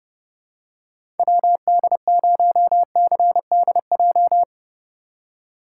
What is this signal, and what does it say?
Morse code, heard as a keyed tone, about 30 wpm, 700 Hz. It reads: WD0CDJ